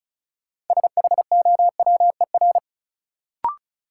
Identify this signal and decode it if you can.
Morse code, heard as a keyed tone, about 36 wpm, 700 Hz. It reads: SHOWER
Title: Sentences from Top 1600 Words - New Word Drill - 35wpm